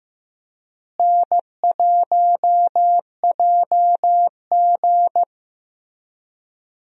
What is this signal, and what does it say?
Morse code, heard as a keyed tone, about 14 wpm, 700 Hz. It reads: N1JG
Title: USA Call Signs - 4 Characters - Repeated 3x - 15wpm